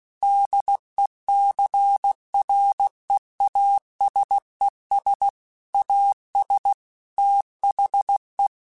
Morse code, heard as a keyed tone, about 16 wpm, 800 Hz. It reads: DECREASES AS THE
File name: Technical Subjects / Technical Subjects-mp3s / XC @15